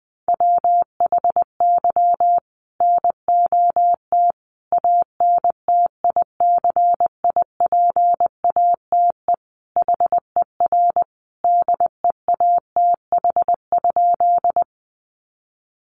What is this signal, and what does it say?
Morse code, heard as a keyed tone, about 20 wpm, 700 Hz. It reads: WHY NOT ANTICIPATE HER DEATH?
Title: Sentences from Top 1500 Words 20wpm